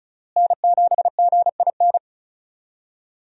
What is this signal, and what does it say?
Morse code, heard as a keyed tone, about 36 wpm, 700 Hz. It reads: N7GIN
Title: USA Call Signs - 5 Characters - Repeated 3x - 35wpm